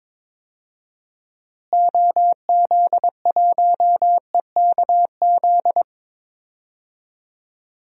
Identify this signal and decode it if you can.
Morse code, heard as a keyed tone, about 22 wpm, 700 Hz. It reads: OZ1EKZ
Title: DXCC-100 Call Signs - 6 Characters - Repeated 3x - 22wpm